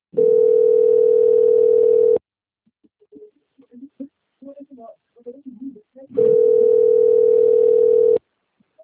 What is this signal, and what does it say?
Ringback tone (North America), 440 + 480 Hz, with speech.
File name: out- - - - - .8313